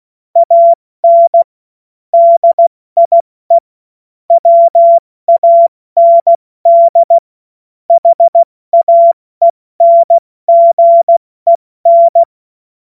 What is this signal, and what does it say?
Morse code, heard as a keyed tone, about 16 wpm, 680 Hz. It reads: AN DIE WAND HAENGEN
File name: Morsekurs / Witze / Witz 28. September 2021, 70 F80